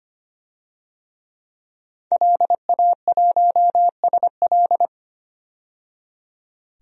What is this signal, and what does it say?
Morse code, heard as a keyed tone, about 25 wpm, 700 Hz. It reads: LA1SL
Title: DXCC-100 Call Signs - 5 Characters - 25wpm